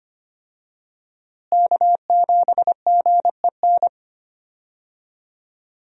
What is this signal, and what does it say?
Morse code, heard as a keyed tone, about 25 wpm, 700 Hz. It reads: K7GEN
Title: USA Call Signs - 5 Characters - Repeated 3x - 25wpm